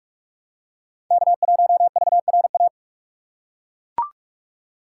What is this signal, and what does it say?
Morse code, heard as a keyed tone, about 41 wpm, 700 Hz. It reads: K1VRA